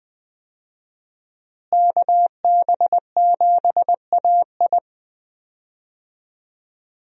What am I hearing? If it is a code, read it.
Morse code, heard as a keyed tone, about 20 wpm, 700 Hz. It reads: KB7AI
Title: USA Call Signs - 5 Characters - Repeated 3x - 20wpm